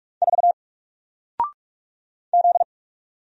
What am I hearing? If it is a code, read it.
Morse code, heard as a keyed tone, about 46 wpm, 700 Hz. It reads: 4 Z
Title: Single Letter-Number Alt 45wpm